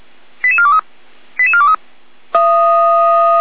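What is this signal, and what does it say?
Five-tone selective call (ZVEI), heard as quick runs of five tones, tones of about 70 ms. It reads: 88423 88423
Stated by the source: Five-tone ZVEI selective call received on an SDR by feuerwehr-orlamuende, fire brigade alerting.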